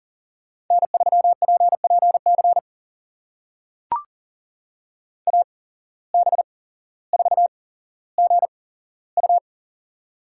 Morse code, heard as a keyed tone, about 40 wpm, spaced as for 17 wpm, 700 Hz. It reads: N3PPC AB4GU
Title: USA Call Signs - 5 Characters - 8x Character Spacing - 40wpm